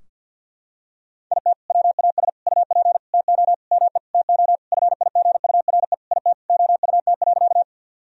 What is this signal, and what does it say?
Morse code, heard as a keyed tone, about 46 wpm, 700 Hz. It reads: IT WAS UP TO ME TO FIGURE IT OUT.
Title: Sentences from Top 1000 Words - Review Part 2 - 50wpm